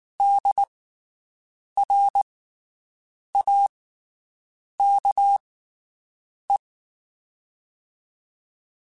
Morse code, heard as a keyed tone, about 19 wpm, spaced as for 7 wpm, 800 Hz. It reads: DRAKE
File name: QSOs Spaced